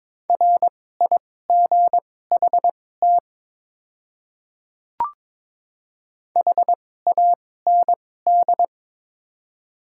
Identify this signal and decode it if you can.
Morse code, heard as a keyed tone, about 22 wpm, spaced as for 16 wpm, 700 Hz. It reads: RIGHT HAND
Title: 2x Character Spacing - Top 300 Words - 22wpm